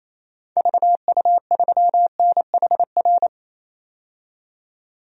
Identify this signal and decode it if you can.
Morse code, heard as a keyed tone, about 28 wpm, 700 Hz. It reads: VU3NHR
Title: DXCC-100 Call Signs - 6 Characters - 28wpm